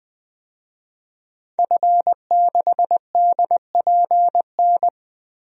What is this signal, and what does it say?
Morse code, heard as a keyed tone, about 20 wpm, 700 Hz. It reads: F6DPN
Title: DXCC-100 Call Signs - 5 Characters - 20wpm